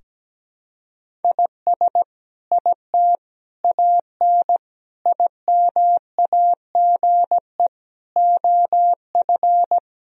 Morse code, heard as a keyed tone, about 17 wpm, 700 Hz. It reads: IS IT AN IMAGE OF
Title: Sentences from Top 1000 Words - Review Part 4 - 17wpm